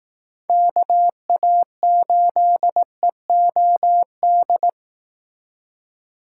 Morse code, heard as a keyed tone, about 19 wpm, 700 Hz. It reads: KA8EOD
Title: Speed Racing - USA Call Signs 15wpm